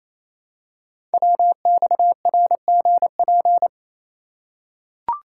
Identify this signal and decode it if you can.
Morse code, heard as a keyed tone, about 29 wpm, 700 Hz. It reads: WXRGP
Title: Code Groups - 5 Letters - 28wpm